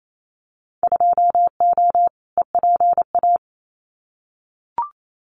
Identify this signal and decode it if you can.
Morse code, heard as a keyed tone, about 28 wpm, 700 Hz. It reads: 2O EPA
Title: Winter Field Day Exchanges Encore 28wpm